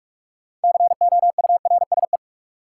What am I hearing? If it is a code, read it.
Morse code, heard as a keyed tone, about 46 wpm, 700 Hz. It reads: COURSE
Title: Sentences from Top 500 Words - New Word Drill - 45wpm